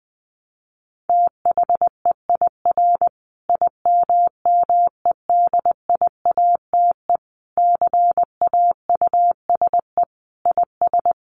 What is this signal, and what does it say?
Morse code, heard as a keyed tone, about 19 wpm, 700 Hz. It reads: THEIR IMMEDIATE CAUSE IS
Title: Sentences from Top 1200 Words 20wpm